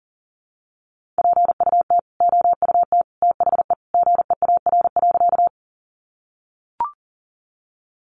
Morse code, heard as a keyed tone, about 40 wpm, 700 Hz. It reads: PUT OUT THE GEAR.
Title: Sentences from Top 1400 Words 40wpm